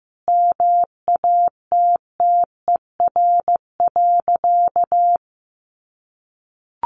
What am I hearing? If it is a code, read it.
Morse code, heard as a keyed tone, about 15 wpm, 700 Hz. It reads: MATTER.
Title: Sentences from Top 1400 Words 15wpm